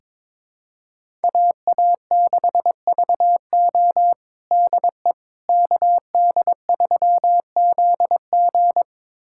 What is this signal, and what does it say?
Morse code, heard as a keyed tone, about 22 wpm, 700 Hz. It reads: AA6VO DE KD3ZG
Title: USA Callsign DE Callsign - 5 Characters - 22wpm